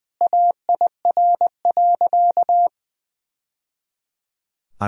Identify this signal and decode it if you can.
Morse code, heard as a keyed tone, about 20 wpm, 700 Hz. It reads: AIR.
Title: Sentences from Top 1000 Words - Review Part 1 - 20wpm